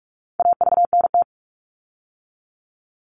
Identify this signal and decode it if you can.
Morse code, heard as a keyed tone, about 44 wpm, 700 Hz. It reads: AVNT
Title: All Cut Numbers - 4-Digit 45wpm